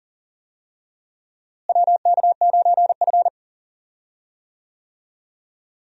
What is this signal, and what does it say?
Morse code, heard as a keyed tone, about 37 wpm, 700 Hz. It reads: WK9F
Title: USA Call Signs - 4 Characters - Repeated 3x - 40wpm